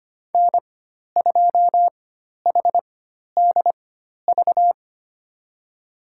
Morse code, heard as a keyed tone, about 25 wpm, spaced as for 12 wpm, 700 Hz. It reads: N2HDV
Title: USA Call Signs - 5 Characters - 4x Character Spacing - 25wpm